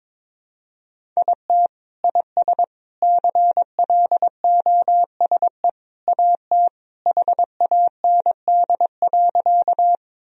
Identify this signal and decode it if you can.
Morse code, heard as a keyed tone, about 22 wpm, 700 Hz. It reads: IT IS CLOSE AT HAND.
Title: Sentences from Top 1000 Words - Review Part 1 - 22wpm